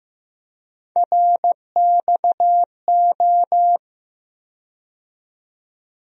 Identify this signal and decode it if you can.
Morse code, heard as a keyed tone, about 15 wpm, 700 Hz. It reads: RXO
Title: USA Call Sign Postfixes - 3 Letters 15wpm